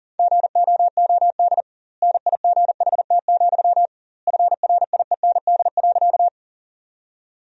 Morse code, heard as a keyed tone, about 40 wpm, 700 Hz. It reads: GOOD NIGHT, FRIEND.
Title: 2x Word-Spacing - Sentences from Top 300 Words - 40wpm